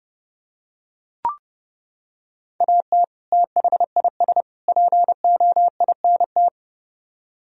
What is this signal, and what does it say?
Morse code, heard as a keyed tone, about 30 wpm, 700 Hz. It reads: AT THIS POINT